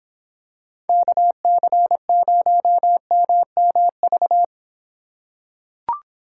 Morse code, heard as a keyed tone, about 26 wpm, 700 Hz. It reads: KC0MMV